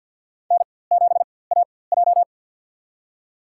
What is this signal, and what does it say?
Morse code, heard as a keyed tone, about 51 wpm, spaced as for 23 wpm, 700 Hz. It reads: N7AJ
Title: USA Call Signs - 4x Character Spacing - 50wpm